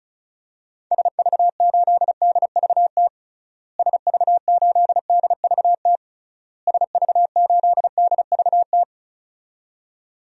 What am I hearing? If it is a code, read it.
Morse code, heard as a keyed tone, about 34 wpm, 700 Hz. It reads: SV8DVT SV8DVT SV8DVT
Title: DXCC-100 Call Signs - 6 Characters - Repeated 3x - 35wpm